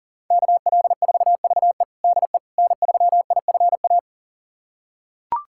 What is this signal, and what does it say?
Morse code, heard as a keyed tone, about 37 wpm, 700 Hz. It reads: KL4VE DE N3IFA
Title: USA Callsign DE Callsign - 5 Characters - 40wpm